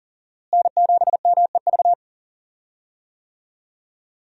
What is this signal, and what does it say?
Morse code, heard as a keyed tone, about 40 wpm, 700 Hz. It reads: N7MEV
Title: USA Call Signs - 5 Characters - Repeated 2x - 40wpm